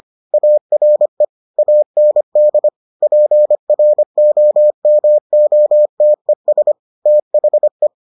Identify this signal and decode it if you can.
Morse code, heard as a keyed tone, about 25 wpm, 600 Hz. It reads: ARE AND PROMOTES THE